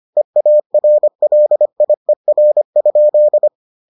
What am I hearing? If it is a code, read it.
Morse code, heard as a keyed tone, about 25 wpm, 600 Hz. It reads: EARLIER?